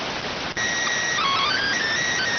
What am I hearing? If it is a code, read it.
SSTV (slow-scan television), PD 290.